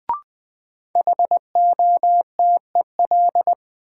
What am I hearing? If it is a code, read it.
Morse code, heard as a keyed tone, about 20 wpm, 700 Hz. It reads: HOTEL
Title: Sentences from Top 1600 Words - New Word Drill - 20wpm